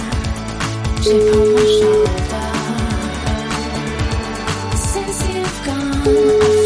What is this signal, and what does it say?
Ringback tone (Europe, 425 Hz), with speech.